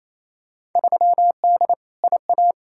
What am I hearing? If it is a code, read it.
Morse code, heard as a keyed tone, about 28 wpm, 700 Hz. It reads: 3D IA
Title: Field Day Exchanges Encore - 28wpm